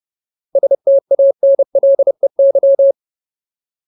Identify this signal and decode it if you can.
Morse code, heard as a keyed tone, about 30 wpm, 550 Hz. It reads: STANLEY
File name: Ultimate Word List / 30WPM Ultimate